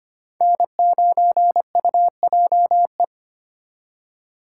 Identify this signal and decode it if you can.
Morse code, heard as a keyed tone, about 32 wpm, 700 Hz. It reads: N9UJE